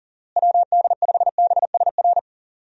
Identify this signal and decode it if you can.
Morse code, heard as a keyed tone, about 40 wpm, 700 Hz. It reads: WD5BSR
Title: USA Call Signs - 6 Characters - Repeated 3x - 40wpm